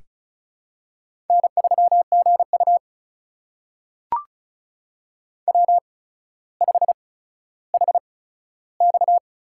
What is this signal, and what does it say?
Morse code, heard as a keyed tone, about 36 wpm, spaced as for 15 wpm, 700 Hz. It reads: N3GU W5HX
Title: USA Call Signs - 4 Characters - 8x Character Spacing - 35wpm